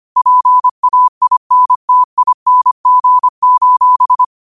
Morse code, heard as a keyed tone, about 25 wpm, 1000 Hz. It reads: PAINTING: